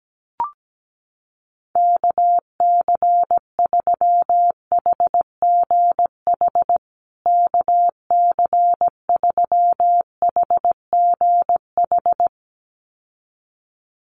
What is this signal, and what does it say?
Morse code, heard as a keyed tone, about 17 wpm, 700 Hz. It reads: KC3HGH KC3HGH
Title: USA Call Signs Repeated 2x 17wpm